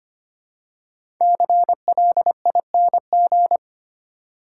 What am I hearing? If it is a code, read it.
Morse code, heard as a keyed tone, about 25 wpm, 700 Hz. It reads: CLING
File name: 5 Letter Words - First 3 Letters Given - 25wpm